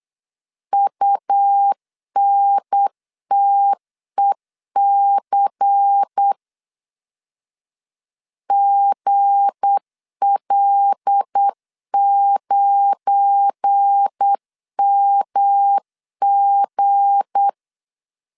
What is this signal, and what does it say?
Morse code, heard as a keyed tone, about 8 wpm, 800 Hz. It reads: UNTEC GL9MG
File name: LEZIONI / 7 GF9Z / 7 GF9 8wpm 3545